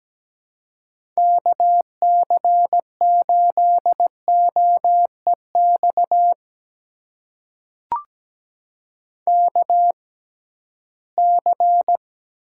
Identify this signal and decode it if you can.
Morse code, heard as a keyed tone, about 17 wpm, spaced as for 13 wpm, 700 Hz. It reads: KC8OEX KC
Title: USA Call Signs - 6 Characters - 6x Character Spacing - 17wpm